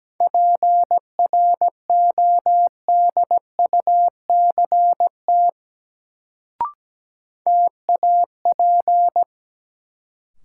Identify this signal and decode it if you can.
Morse code, heard as a keyed tone, about 17 wpm, 700 Hz. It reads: PRODUCT TAP
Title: Sentences from Top 1400 Words - New Word Drill Alt - 17wpm